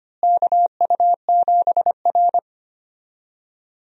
Morse code, heard as a keyed tone, about 25 wpm, 700 Hz. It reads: KU7R
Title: USA Call Signs - 4 Characters - Repeated 2x - 25wpm